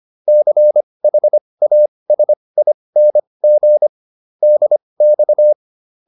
Morse code, heard as a keyed tone, about 25 wpm, spaced as for 20 wpm, 600 Hz. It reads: CHASING DX
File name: SS306 20